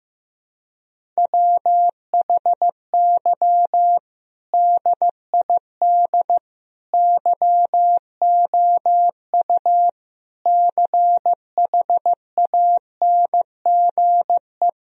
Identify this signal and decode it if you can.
Morse code, heard as a keyed tone, about 15 wpm, 700 Hz. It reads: WHY DID YOU CHANGE
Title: Sentences from Top 1000 Words - Review Part 1 - 15wpm